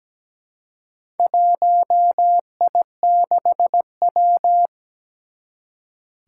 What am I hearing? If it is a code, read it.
Morse code, heard as a keyed tone, about 17 wpm, 700 Hz. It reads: 1I6W
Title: Four Challenging Letter-Number Combinations 17wpm